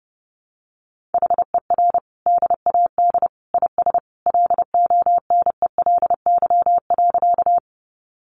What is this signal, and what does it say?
Morse code, heard as a keyed tone, about 30 wpm, 700 Hz. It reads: HER DAD IS LONELY.
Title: Additional Easy Phrases 30wpm